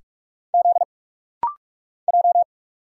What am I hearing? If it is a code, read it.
Morse code, heard as a keyed tone, about 46 wpm, 700 Hz. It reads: Z J
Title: Single Letters - Rapid-Fire 45wpm